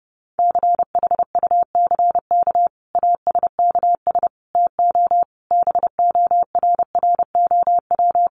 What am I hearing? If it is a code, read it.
Morse code, heard as a keyed tone, about 30 wpm, 700 Hz. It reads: CHUCK ASKS TO BORROW
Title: Additional Easy Phrases 30wpm